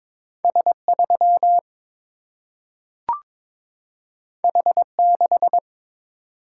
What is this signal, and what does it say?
Morse code, heard as a keyed tone, about 22 wpm, 700 Hz. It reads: S3 H6
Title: Single-Letter Single-Number 22wpm